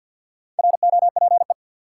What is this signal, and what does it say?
Morse code, heard as a keyed tone, about 52 wpm, 700 Hz. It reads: ROPE
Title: Four Letter Words - 50wpm